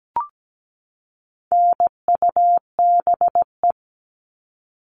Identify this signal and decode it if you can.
Morse code, heard as a keyed tone, about 17 wpm, 700 Hz. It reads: NUBE
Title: All Cut Numbers - 4-Digit 17wpm